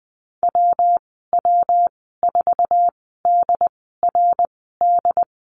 Morse code, heard as a keyed tone, about 20 wpm, spaced as for 15 wpm, 700 Hz. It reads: WW4DRD
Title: 167 Lesson 27g - Number 7 - Call Signs 20wpm